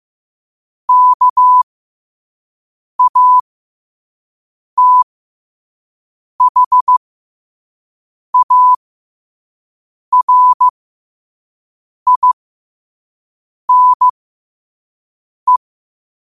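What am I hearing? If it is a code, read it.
Morse code, heard as a keyed tone, about 15 wpm, spaced as for 5 wpm, 1000 Hz. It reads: KATHARINE
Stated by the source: Morse code quote of the day at 5 wpm.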